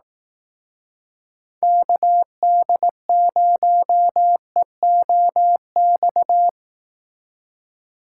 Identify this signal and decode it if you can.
Morse code, heard as a keyed tone, about 18 wpm, 700 Hz. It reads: KD0EOX
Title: Speed Racing - USA Call Signs 15wpm